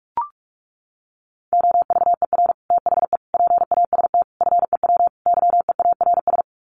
Morse code, heard as a keyed tone, about 44 wpm, 700 Hz. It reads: OVER THE PAST FEW YEARS
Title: Sets of 5 Words Encore 45wpm